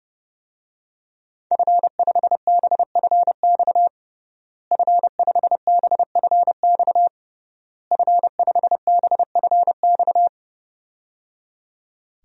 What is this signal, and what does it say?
Morse code, heard as a keyed tone, about 30 wpm, 700 Hz. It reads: F5BFX F5BFX F5BFX
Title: DXCC-100 Call Signs - 5 Characters - Repeated 3x - 30wpm